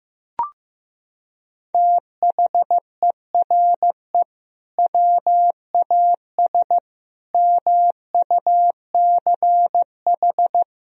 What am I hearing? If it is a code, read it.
Morse code, heard as a keyed tone, about 16 wpm, 700 Hz. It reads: THERE WAS MUCH
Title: Sentences from Top 1000 Words - Review Part 4 - 15wpm